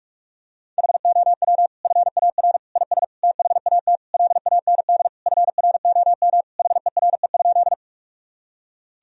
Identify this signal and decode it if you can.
Morse code, heard as a keyed tone, about 46 wpm, 700 Hz. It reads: HOW FAR IS THAT LAND FROM HERE?
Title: Sentences from Top 1000 Words - Review Part 1 - 45wpm